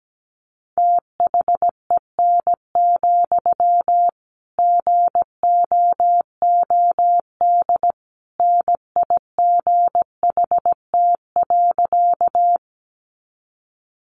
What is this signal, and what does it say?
Morse code, heard as a keyed tone, about 17 wpm, 700 Hz. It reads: THEN, GOOD NIGHT.
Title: Sentences from Top 300 Words 17wpm